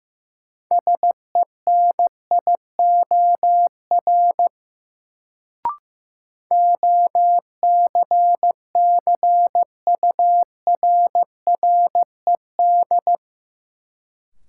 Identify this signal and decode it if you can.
Morse code, heard as a keyed tone, about 16 wpm, 700 Hz. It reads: SENIOR OCCURRED
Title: Sets of 5 Words Encore - New Word Drill - 15wpm